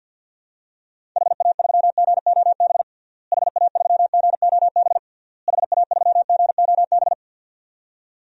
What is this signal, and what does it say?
Morse code, heard as a keyed tone, about 53 wpm, 700 Hz. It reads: HA3GOB HA3GOB HA3GOB
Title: DXCC-100 Call Signs - 6 Characters - Repeated 3x - 50wpm